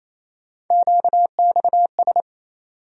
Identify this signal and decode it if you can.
Morse code, heard as a keyed tone, about 28 wpm, 700 Hz. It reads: QXS
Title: USA Call Sign Postfixes - 3 Letters 28wpm